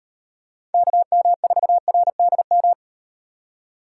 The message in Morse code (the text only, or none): KM4RDM